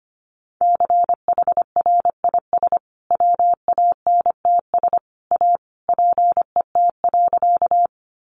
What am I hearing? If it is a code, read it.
Morse code, heard as a keyed tone, about 25 wpm, 700 Hz. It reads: CHRIS WANTS A PET.
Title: Additional Easy Phrases 25wpm